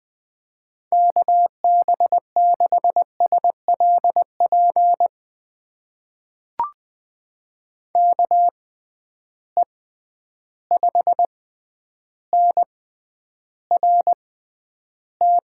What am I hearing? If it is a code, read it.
Morse code, heard as a keyed tone, about 20 wpm, spaced as for 10 wpm, 700 Hz. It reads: KB6SLP KE5NRT
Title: USA Call Signs - 6 Characters - 6x Character Spacing - 20wpm